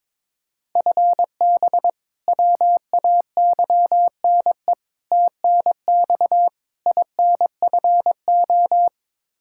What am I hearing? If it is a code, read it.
Morse code, heard as a keyed tone, about 22 wpm, 700 Hz. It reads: FB WAYNE TNX INFO
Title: Practice QSO - Exchange 3 - More Info - 22wpm